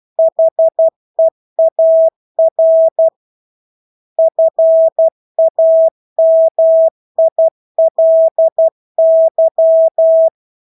Morse code, heard as a keyed tone, about 12 wpm, 650 Hz. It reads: HEAR FAMILY
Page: Engelske ord 12 wpm